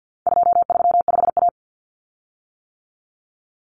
Morse code, heard as a keyed tone, about 50 wpm, 700 Hz. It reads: 235A
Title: Common Cut Numbers - 4-Digit 50wpm